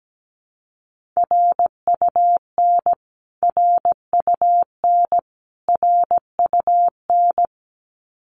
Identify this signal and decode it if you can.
Morse code, heard as a keyed tone, about 17 wpm, 700 Hz. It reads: RUN RUN RUN
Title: Top 500 Words Repeated 17wpm